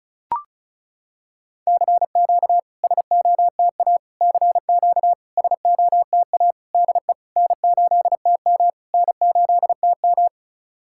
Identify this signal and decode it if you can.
Morse code, heard as a keyed tone, about 35 wpm, 700 Hz. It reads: CQ SOTA CQ SOTA DE N8TM N8TM
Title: SOTA QSO Practice 35wpm